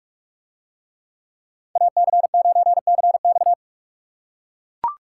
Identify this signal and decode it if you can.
Morse code, heard as a keyed tone, about 40 wpm, 700 Hz. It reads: AC9CX